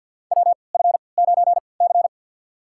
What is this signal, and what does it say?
Morse code, heard as a keyed tone, about 53 wpm, spaced as for 29 wpm, 700 Hz. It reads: WF9C
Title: USA Call Signs - 4 Characters - 3x Character Spacing - 50wpm